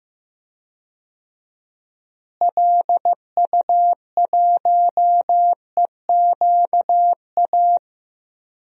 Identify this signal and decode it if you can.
Morse code, heard as a keyed tone, about 16 wpm, 700 Hz. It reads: LU1EQA
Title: DXCC-100 Call Signs - 6 Characters - Repeated 3x - 15wpm